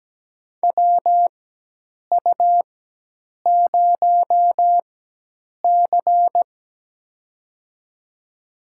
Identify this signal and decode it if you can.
Morse code, heard as a keyed tone, about 17 wpm, spaced as for 8 wpm, 700 Hz. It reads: WU0C
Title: USA Call Signs - 4 Characters - 4x Character Spacing - 17wpm